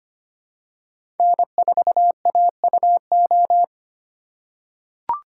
Morse code, heard as a keyed tone, about 25 wpm, 700 Hz. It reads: N4AUO
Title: USA Call Signs - 5 Characters - 8x Character Spacing - 25wpm